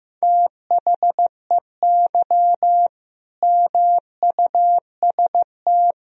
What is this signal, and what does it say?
Morse code, heard as a keyed tone, about 15 wpm, 700 Hz. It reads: THEY MUST
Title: Sentences from Top 1000 Words - Review Part 1 - 15wpm